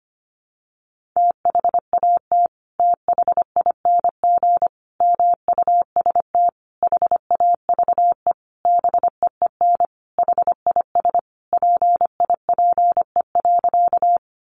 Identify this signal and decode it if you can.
Morse code, heard as a keyed tone, about 25 wpm, 700 Hz. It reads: THAT THING MUST HAVE BEEN HIS PIPE.
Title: Sentences from Top 1500 Words 25wpm